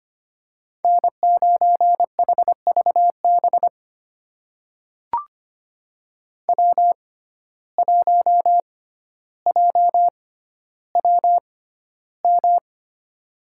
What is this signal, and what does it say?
Morse code, heard as a keyed tone, about 25 wpm, spaced as for 13 wpm, 700 Hz. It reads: N9HVB W1JWM
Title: USA Call Signs - 6x Character Spacing - 25wpm